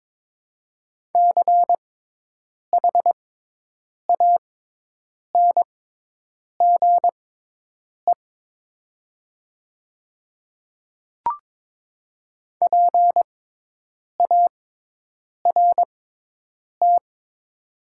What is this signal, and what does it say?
Morse code, heard as a keyed tone, about 22 wpm, spaced as for 8 wpm, 700 Hz. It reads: CHANGE PART
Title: 6x Character Spacing - Top 300 Words - 22wpm